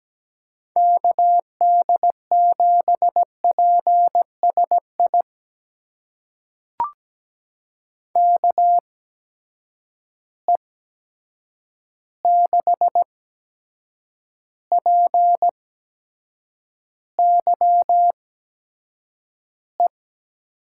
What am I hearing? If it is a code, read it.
Morse code, heard as a keyed tone, about 17 wpm, spaced as for 7 wpm, 700 Hz. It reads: KD7PSI KE6PYE
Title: USA Call Signs - 6 Characters - 8x Character Spacing - 17wpm